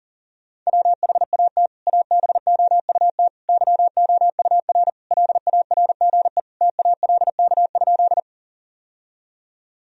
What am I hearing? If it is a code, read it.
Morse code, heard as a keyed tone, about 37 wpm, 700 Hz. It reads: WHAT ABOUT YOUR LARGE TALK?
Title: Sentences from Top 1000 Words - Review Part 1 - 40wpm